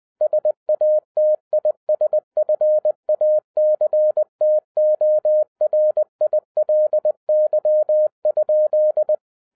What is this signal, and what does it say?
Morse code, heard as a keyed tone, about 20 wpm, 610 Hz. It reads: SATISFACTORILY?